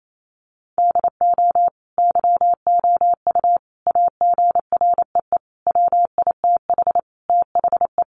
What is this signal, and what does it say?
Morse code, heard as a keyed tone, about 28 wpm, 700 Hz. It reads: DO YOU AGREE WITH THE
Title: Sentences from Top 1400 Words 28wpm